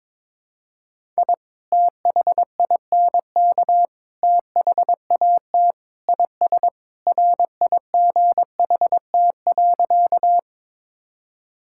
Morse code, heard as a keyed tone, about 22 wpm, 700 Hz. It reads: I THINK THAT IS RIGHT.
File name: practice / 22wpm / Sentences with 3-Word Phrases - 22wpm